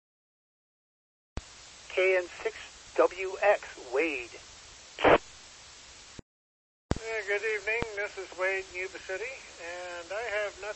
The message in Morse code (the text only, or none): none